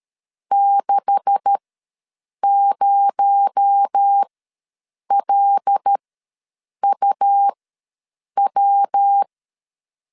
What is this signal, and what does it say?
Morse code, heard as a keyed tone, about 13 wpm, spaced as for 7 wpm, 800 Hz. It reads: 60LUW
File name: LEZIONI / 9 286 ws 12wpm 3351